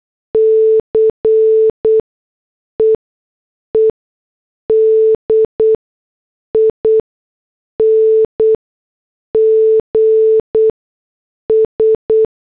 Morse code, heard as a keyed tone, about 8 wpm, 440 Hz. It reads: C E E D I N G S